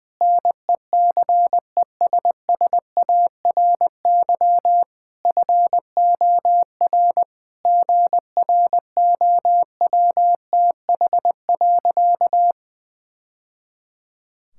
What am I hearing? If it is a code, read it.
Morse code, heard as a keyed tone, about 20 wpm, 700 Hz. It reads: NECESSARY FOR GROWTH.
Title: Sentences from Top 1000 Words - Review Part 5 - 20wpm